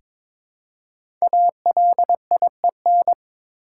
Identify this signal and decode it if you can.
Morse code, heard as a keyed tone, about 22 wpm, 700 Hz. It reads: ALIEN
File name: 5 Letter Words - First 3 Letters Given - 22wpm